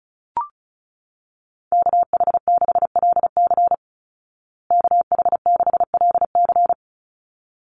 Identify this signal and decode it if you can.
Morse code, heard as a keyed tone, about 34 wpm, 700 Hz. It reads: KH6LC KH6LC
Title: Call Signs Repeated 3x - 35wpm